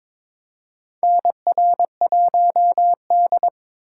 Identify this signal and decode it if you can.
Morse code, heard as a keyed tone, about 23 wpm, 700 Hz. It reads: NR1D
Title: USA Call Signs - 4 Characters - 6x Character Spacing - 22wpm